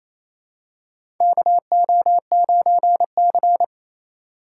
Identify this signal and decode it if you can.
Morse code, heard as a keyed tone, about 27 wpm, 700 Hz. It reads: KO9C
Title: USA Call Signs - 4 Characters - 4x Character Spacing - 28wpm